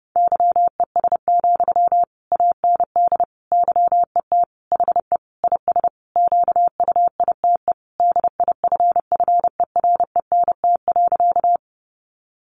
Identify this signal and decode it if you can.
Morse code, heard as a keyed tone, about 30 wpm, 700 Hz. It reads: YES, AND YET HE IS QUITE DIFFERENT.
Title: Sentences from Top 700 Words 30wpm